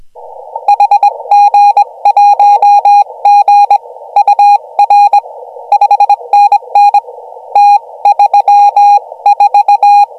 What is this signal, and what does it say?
Morse code, heard as a keyed tone, about 21 wpm, spaced as for 19 wpm, 820 Hz. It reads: HG1G UR 5NN T34
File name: AUDIO2019 / HG1G CW